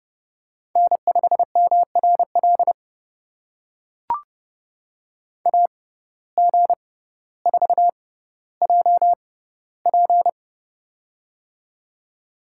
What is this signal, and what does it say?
Morse code, heard as a keyed tone, about 29 wpm, spaced as for 15 wpm, 700 Hz. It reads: N5MRL AG4JP
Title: USA Call Signs - 5 Characters - 6x Character Spacing - 30wpm